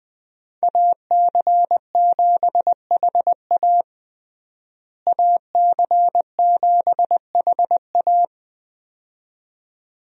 Morse code, heard as a keyed tone, about 20 wpm, 700 Hz. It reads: AC7HA AC7HA
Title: USA Call Signs - 5 Characters - Repeated 2x - 20wpm